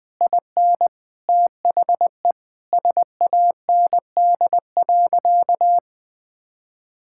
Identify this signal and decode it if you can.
Morse code, heard as a keyed tone, about 21 wpm, 700 Hz. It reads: IN THE SAND.
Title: Sentences with 7-Word Phrases - 20wpm